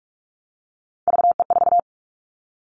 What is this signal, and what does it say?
Morse code, heard as a keyed tone, about 46 wpm, 700 Hz. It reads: VE4
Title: Callsign Prefixes for Top 25 DXCC - 45wpm